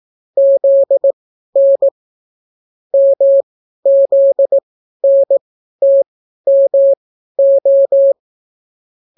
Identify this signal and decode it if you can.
Morse code, heard as a keyed tone, about 18 wpm, spaced as for 12 wpm, 550 Hz. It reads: ZN MZNTMO